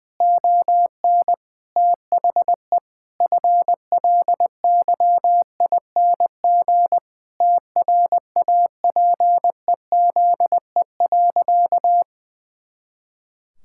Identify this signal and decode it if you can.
Morse code, heard as a keyed tone, about 20 wpm, 700 Hz. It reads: ON THE FLYING TRAPEZE.
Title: Pangram Challenge v2 - 20wpm